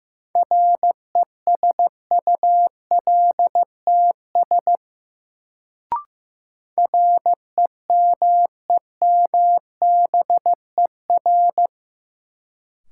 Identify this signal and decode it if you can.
Morse code, heard as a keyed tone, about 15 wpm, 700 Hz. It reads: RESULTS REMEMBER
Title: Sets of 4 Words - New Word Drill - 15wpm